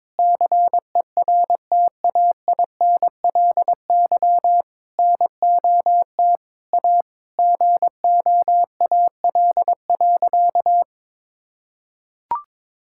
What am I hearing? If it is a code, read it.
Morse code, heard as a keyed tone, about 23 wpm, 700 Hz. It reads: CERTAINLY NOT A GOAL.
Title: Sentences from Top 1000 Words - Review Part 5 - 22wpm